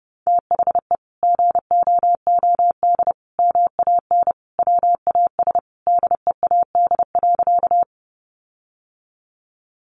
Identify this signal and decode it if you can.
Morse code, heard as a keyed tone, about 30 wpm, 700 Hz. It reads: THE GOOD MAN WAS DEAD.
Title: Sentences from Top 1500 Words 30wpm